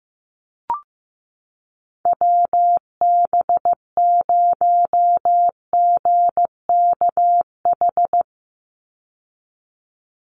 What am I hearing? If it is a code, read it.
Morse code, heard as a keyed tone, about 15 wpm, 700 Hz. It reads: WB0GKH
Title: Call Signs Repeated 3x - 15wpm